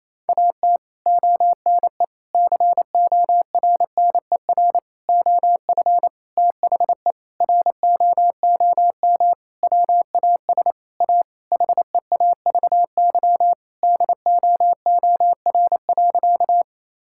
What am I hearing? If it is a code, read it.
Morse code, heard as a keyed tone, about 28 wpm, 700 Hz. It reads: AT ONE CORNER OF THE ROOM WAS A HEAVY DOOR.
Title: Sentences from Top 1600 Words 28wpm